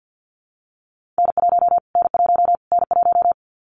Morse code, heard as a keyed tone, about 50 wpm, 700 Hz. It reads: N1 N1 N1
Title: Callsign Prefixes for Top 25 DXCC - Repeated 3x - 50wpm